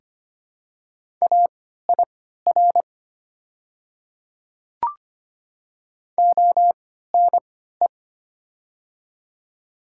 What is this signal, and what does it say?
Morse code, heard as a keyed tone, about 25 wpm, spaced as for 14 wpm, 700 Hz. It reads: AIR ONE
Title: 3x Character Spacing - Top 200 Words - 25wpm